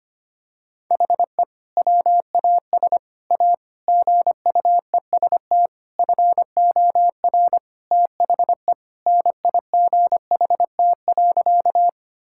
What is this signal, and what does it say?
Morse code, heard as a keyed tone, about 25 wpm, 700 Hz. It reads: HE WAS A GUEST FOR THE NIGHT.
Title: Sentences from Top 1600 Words 25wpm